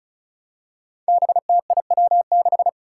Morse code, heard as a keyed tone, about 34 wpm, 700 Hz. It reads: BTIW6
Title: Five Challenging Letter-Number Combinations 35wpm